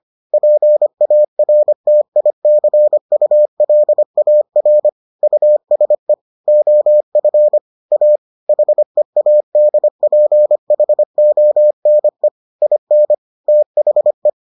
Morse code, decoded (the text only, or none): PARTICULAR USE OF A HEADPHONE IN THE